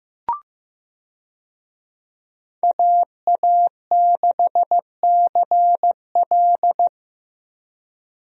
Morse code, heard as a keyed tone, about 16 wpm, 700 Hz. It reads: AA6CL
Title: USA Call Signs - 5 Characters - Repeated 2x - 15wpm